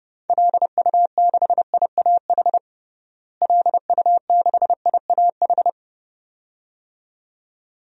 Morse code, heard as a keyed tone, about 30 wpm, 700 Hz. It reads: LU6IAH LU6IAH
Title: DXCC-100 Call Signs - 6 Characters - Repeated 2x - 30wpm